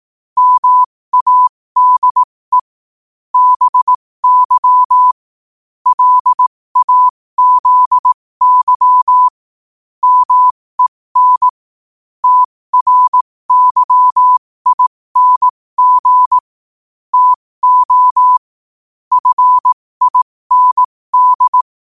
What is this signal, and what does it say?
Morse code, heard as a keyed tone, about 18 wpm, spaced as for 15 wpm, 1000 Hz. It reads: MADE BY LAZY MEN TRYING TO FIND
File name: qotdfiles / 15wpm